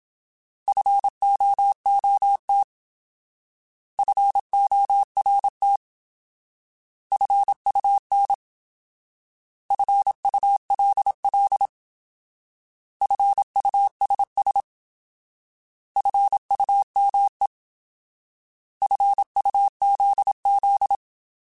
Morse code, heard as a keyed tone, about 27 wpm, 800 Hz. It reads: FOOT FORT FUN FULL FUSS FUME FUZZ
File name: Alphabetical / Alphabetical mp3s / Fs @25